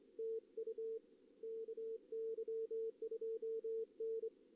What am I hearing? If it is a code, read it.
Morse code, heard as a keyed tone, about 26 wpm, 450 Hz. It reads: TU KY2N
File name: contests / CW-SS-k5tr- -1828